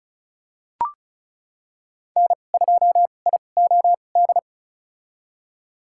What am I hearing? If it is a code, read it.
Morse code, heard as a keyed tone, about 34 wpm, spaced as for 25 wpm, 700 Hz. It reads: N2IOD
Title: USA Call Signs - 2x Character Spacing - 35wpm